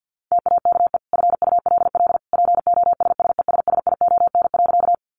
Morse code, heard as a keyed tone, about 50 wpm, 700 Hz. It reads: TAKE FULL POSSESSION.
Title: Sentences from Top 1400 Words 50wpm